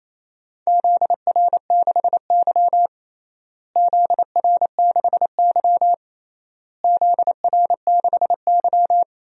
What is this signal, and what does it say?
Morse code, heard as a keyed tone, about 28 wpm, 700 Hz. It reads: ZR6Y ZR6Y ZR6Y
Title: DXCC-100 Call Signs - 4 Characters - Repeated 3x - 28wpm